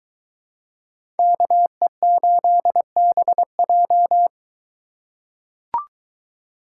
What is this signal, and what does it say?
Morse code, heard as a keyed tone, about 23 wpm, 700 Hz. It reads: KE8BJ